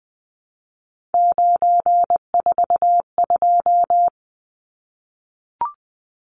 Morse code, heard as a keyed tone, about 20 wpm, 700 Hz. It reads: 942
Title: Numbers - 3-Digit 20wpm